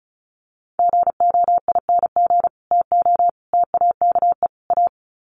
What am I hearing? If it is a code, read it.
Morse code, heard as a keyed tone, about 34 wpm, 700 Hz. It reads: GOING TO TAKE A
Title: Sets of 4 Words 35wpm